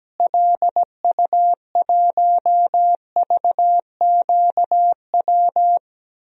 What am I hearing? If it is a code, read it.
Morse code, heard as a keyed tone, about 17 wpm, 700 Hz. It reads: LU1VQW
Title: DXCC-100 Call Signs - 6 Characters - Repeated 2x - 17wpm